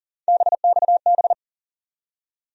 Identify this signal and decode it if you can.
Morse code, heard as a keyed tone, about 40 wpm, 700 Hz. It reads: BXB
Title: USA Call Sign Postfixes - 3 Letters 40wpm